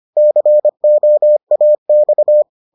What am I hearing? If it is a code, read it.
Morse code, heard as a keyed tone, about 25 wpm, 600 Hz. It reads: COAX